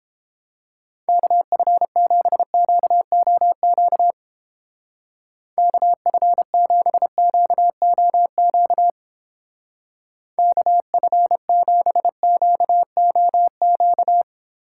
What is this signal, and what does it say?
Morse code, heard as a keyed tone, about 29 wpm, 700 Hz. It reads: KF7QOQ KF7QOQ KF7QOQ